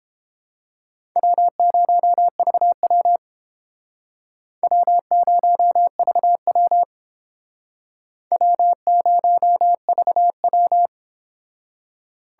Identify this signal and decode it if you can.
Morse code, heard as a keyed tone, about 29 wpm, 700 Hz. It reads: W0VW W0VW W0VW